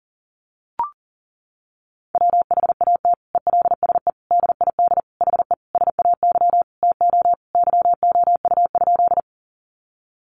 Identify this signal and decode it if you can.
Morse code, heard as a keyed tone, about 40 wpm, 700 Hz. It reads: WHAT ELSE DID HE SAY TO YOU?
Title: Sentences from Top 800 Words 40wpm